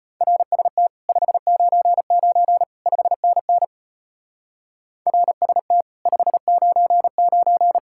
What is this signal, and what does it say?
Morse code, heard as a keyed tone, about 31 wpm, 700 Hz. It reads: RST 599 5NN RST 599
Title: Speed-Racing Common QSO Phrases - 25wpm